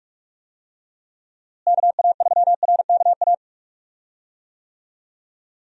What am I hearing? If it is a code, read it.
Morse code, heard as a keyed tone, about 46 wpm, 700 Hz. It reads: KA3RKA
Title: USA Call Signs - 6 Characters - Repeated 2x - 45wpm